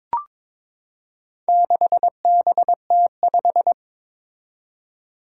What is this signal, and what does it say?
Morse code, heard as a keyed tone, about 22 wpm, 700 Hz. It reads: 6BT5
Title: Four Challenging Letter-Number Combinations 22wpm